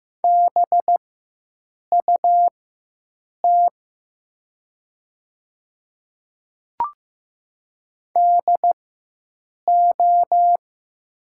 Morse code, heard as a keyed tone, about 15 wpm, spaced as for 7 wpm, 700 Hz. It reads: BUT DO
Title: 4x Character Spacing - Top 200 Words - 15wpm